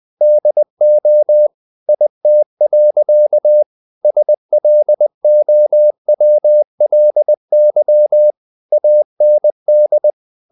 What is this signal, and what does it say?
Morse code, heard as a keyed tone, about 20 wpm, 600 Hz. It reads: DO IT. SLOWLY AND
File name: the old man and the sea / the old man and the sea part 1 20wpm